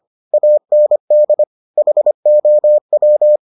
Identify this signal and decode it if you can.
Morse code, heard as a keyed tone, about 25 wpm, 600 Hz. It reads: AND HOW